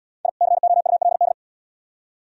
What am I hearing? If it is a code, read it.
Morse code, heard as a keyed tone, about 77 wpm, 700 Hz. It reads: I COULD